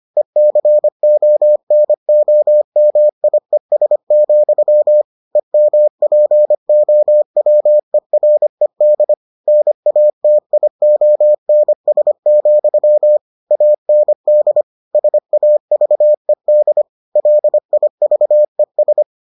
Morse code, heard as a keyed tone, about 25 wpm, 600 Hz. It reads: ECONOMIES, EMPOWERED NATIONS, AND SAVED LIVES